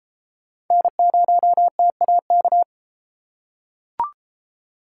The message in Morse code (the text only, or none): N0TAK